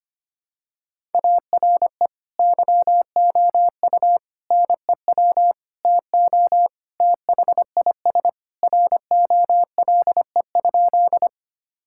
Morse code, heard as a keyed tone, about 25 wpm, 700 Hz. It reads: ARE YOU NEW TO THIS ROLE?
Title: Sentences from Top 1000 Words - Review Part 6 - 25wpm